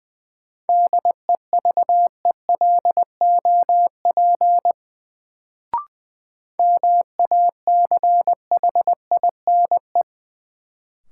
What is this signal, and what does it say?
Morse code, heard as a keyed tone, about 20 wpm, 700 Hz. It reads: DEVELOP MACHINE
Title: Sentences from Top 1000 Words - Review Part 2 - New Word Drill - 20wpm